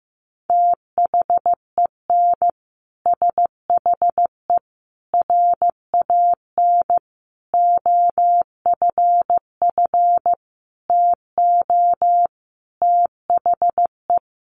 Morse code, heard as a keyed tone, about 15 wpm, 700 Hz. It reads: THEN SHE RAN OFF TO THE
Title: Sentences from Top 1400 Words 15wpm